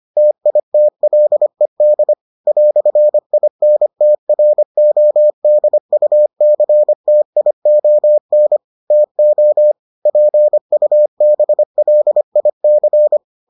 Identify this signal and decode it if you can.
Morse code, heard as a keyed tone, about 25 wpm, 600 Hz. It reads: TITLED "INTRODUCTION TO PUBLIC